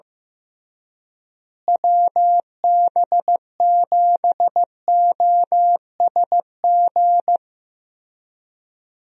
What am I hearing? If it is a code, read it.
Morse code, heard as a keyed tone, about 15 wpm, 700 Hz. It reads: WB7OSG
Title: USA Call Signs - 6 Characters - Repeated 3x - 15wpm